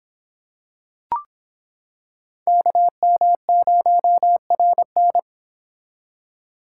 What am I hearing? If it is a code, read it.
Morse code, heard as a keyed tone, about 25 wpm, 700 Hz. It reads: KM0RN